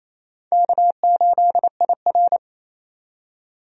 Morse code, heard as a keyed tone, about 28 wpm, 700 Hz. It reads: K8IR
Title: USA Call Signs - 4 Characters - 28wpm